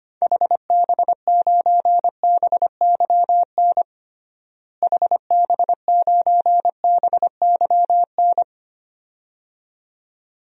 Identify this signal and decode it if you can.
Morse code, heard as a keyed tone, about 25 wpm, 700 Hz. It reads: HB9BYN HB9BYN
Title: DXCC-100 Call Signs - 6 Characters - Repeated 2x - 25wpm